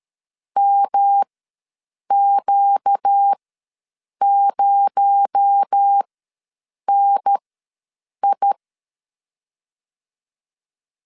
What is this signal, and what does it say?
Morse code, heard as a keyed tone, about 13 wpm, spaced as for 7 wpm, 800 Hz. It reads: MQ0NI
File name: LEZIONI / 7 GF9Z / 7 G ws 12wpm 3140